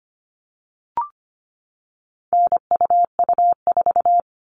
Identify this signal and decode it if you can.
Morse code, heard as a keyed tone, about 25 wpm, 700 Hz. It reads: NUU4
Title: All Cut Numbers - 4-Digit 25wpm